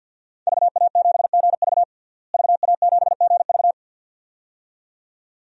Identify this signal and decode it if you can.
Morse code, heard as a keyed tone, about 52 wpm, 700 Hz. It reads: VA7GV VA7GV
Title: DXCC-100 Call Signs - 5 Characters - Repeated 2x - 50wpm